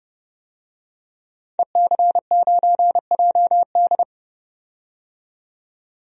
Morse code, heard as a keyed tone, about 30 wpm, 690 Hz. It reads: EC9JD
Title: DXCC-100 Call Signs - 5 Characters - 30wpm